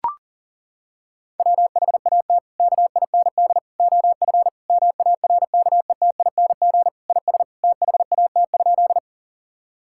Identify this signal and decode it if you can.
Morse code, heard as a keyed tone, about 40 wpm, 700 Hz. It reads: WHAT KIND OF MARKETING IS THAT?
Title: Sentences from Top 1000 Words - Review Part 5 - 40wpm